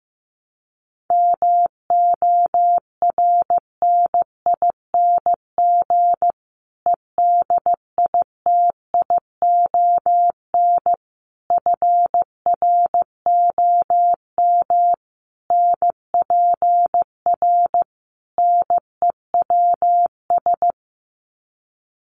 Morse code, heard as a keyed tone, about 15 wpm, 700 Hz. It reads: MORNING EDITION FROM NPR NEWS
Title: Sets of 5 Words Encore 15wpm